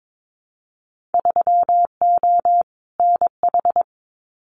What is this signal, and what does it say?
Morse code, heard as a keyed tone, about 22 wpm, 700 Hz. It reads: 3O NH
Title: Winter Field Day Exchanges Encore 22wpm